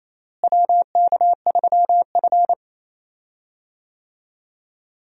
Morse code, heard as a keyed tone, about 28 wpm, 700 Hz. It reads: WK3F
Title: USA Call Signs - 4 Characters - Repeated 2x - 28wpm